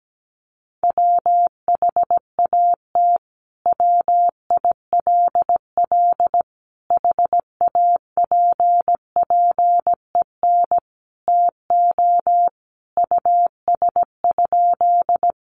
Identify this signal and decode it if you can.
Morse code, heard as a keyed tone, about 17 wpm, 700 Hz. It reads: WHAT WILL HAPPEN TO US?
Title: Sentences from Top 400 Words 17wpm